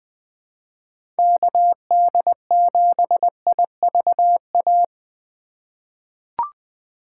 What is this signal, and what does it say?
Morse code, heard as a keyed tone, about 20 wpm, 700 Hz. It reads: KD7IVA
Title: USA Call Signs - 6 Characters - 6x Character Spacing - 20wpm